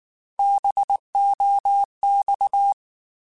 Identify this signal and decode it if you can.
Morse code, heard as a keyed tone, about 19 wpm, 800 Hz. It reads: BOX
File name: Alphabetical / Alphabetical mp3s / Bs @18